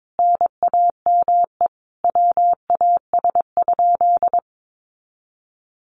Morse code, heard as a keyed tone, about 22 wpm, 700 Hz. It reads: NAME WAS?
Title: Sentences from Top 200 Words 22wpm